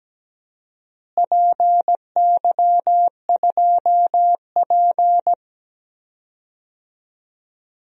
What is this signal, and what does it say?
Morse code, heard as a keyed tone, about 17 wpm, 700 Hz. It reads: PY2P
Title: DXCC-100 Call Signs - 4 Characters - Repeated 3x - 17wpm